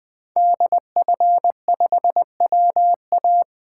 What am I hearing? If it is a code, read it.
Morse code, heard as a keyed tone, about 20 wpm, 700 Hz. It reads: DF5WA
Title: 2023 SOTA Chaser Call Signs - 20wpm